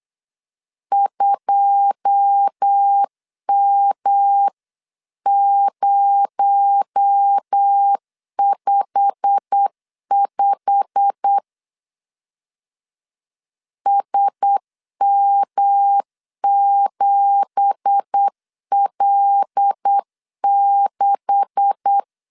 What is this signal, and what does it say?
Morse code, heard as a keyed tone, about 8 wpm, 800 Hz. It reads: 2M055 SM7L6
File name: CW / DET / 1 9 1500C S C 8wpm 5653